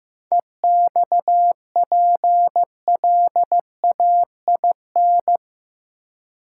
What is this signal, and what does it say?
Morse code, heard as a keyed tone, about 15 wpm, 700 Hz. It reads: EXPLAIN
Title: Sentences from Top 1100 Words - New Word Drill Alt - 15wpm